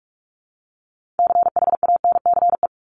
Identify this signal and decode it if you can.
Morse code, heard as a keyed tone, about 44 wpm, 700 Hz. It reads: CHANCE
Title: Top 501-750 Words 45wpm